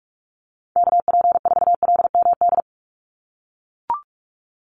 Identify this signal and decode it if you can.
Morse code, heard as a keyed tone, about 44 wpm, 700 Hz. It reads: KP4LMD
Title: USA Call Signs 45wpm